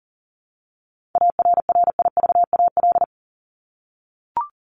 Morse code, heard as a keyed tone, about 40 wpm, 700 Hz. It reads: ARRIVAL
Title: Top 1751-2000 Words 40wpm